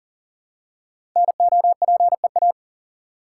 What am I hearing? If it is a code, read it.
Morse code, heard as a keyed tone, about 39 wpm, 700 Hz. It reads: NOPEA
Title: Code Groups - 5 Letters - 40wpm